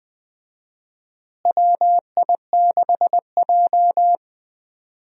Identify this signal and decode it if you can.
Morse code, heard as a keyed tone, about 20 wpm, 700 Hz. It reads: WI6J
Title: USA Call Signs - 4 Characters - Repeated 2x - 20wpm